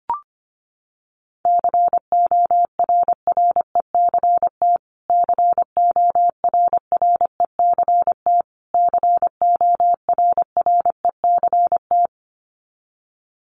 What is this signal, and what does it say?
Morse code, heard as a keyed tone, about 25 wpm, 700 Hz. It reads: CORRECT CORRECT CORRECT
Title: Top 500 Words Repeated 25wpm